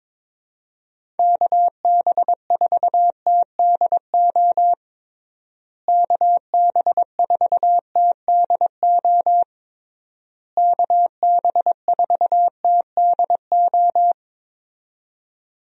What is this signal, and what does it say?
Morse code, heard as a keyed tone, about 22 wpm, 700 Hz. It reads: KB4TDO KB4TDO KB4TDO
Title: USA Call Signs - 6 Characters - Repeated 3x - 22wpm